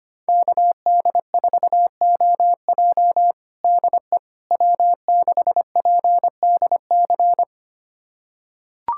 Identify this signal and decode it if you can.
Morse code, heard as a keyed tone, about 25 wpm, 700 Hz. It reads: KD4OJ DE W6PDC
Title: USA Callsign DE Callsign - 5 Characters - 25wpm